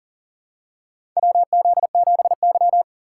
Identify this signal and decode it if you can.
Morse code, heard as a keyed tone, about 42 wpm, 700 Hz. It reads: WZ7Y
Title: USA Call Signs - 4 Characters - 6x Character Spacing - 40wpm